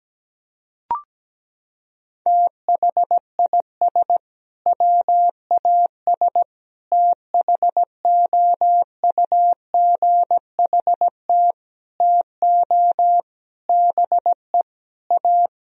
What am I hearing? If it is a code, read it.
Morse code, heard as a keyed tone, about 17 wpm, 700 Hz. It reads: THIS WAS THOUGHT TO BE A
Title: Sentences from Top 1000 Words - Review Part 1 - 17wpm